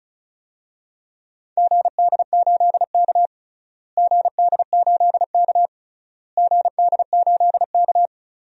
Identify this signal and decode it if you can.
Morse code, heard as a keyed tone, about 34 wpm, 700 Hz. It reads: GD8K GD8K GD8K
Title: DXCC-100 Call Signs - 4 Characters - Repeated 3x - 35wpm